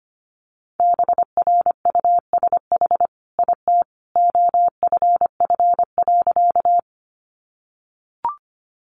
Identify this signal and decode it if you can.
Morse code, heard as a keyed tone, about 25 wpm, 700 Hz. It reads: BRUSH IT OFF.
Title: Sentences from Top 1100 Words 25wpm